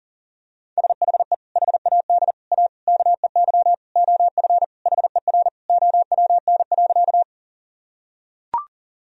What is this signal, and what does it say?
Morse code, heard as a keyed tone, about 40 wpm, 700 Hz. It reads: SHE HAD A KEY OF HER OWN.
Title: Sentences from Top 1000 Words - Review Part 4 - 40wpm